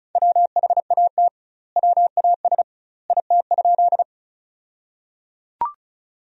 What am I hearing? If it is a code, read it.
Morse code, heard as a keyed tone, about 36 wpm, 700 Hz. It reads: WHAT WAS IT?
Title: 2x Word-Spacing - Sentences from Top 100 Words - 35wpm